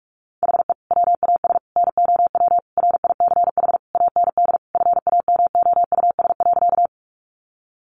Morse code, heard as a keyed tone, about 44 wpm, 700 Hz. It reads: HE WAS NOW RICH AND FAMOUS.
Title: Sentences from Top 1500 Words 45wpm